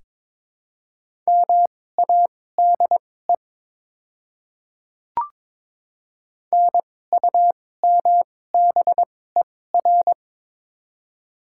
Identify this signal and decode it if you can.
Morse code, heard as a keyed tone, about 22 wpm, spaced as for 16 wpm, 700 Hz. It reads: MADE NUMBER
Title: 2x Character Spacing - Top 200 Words - 22wpm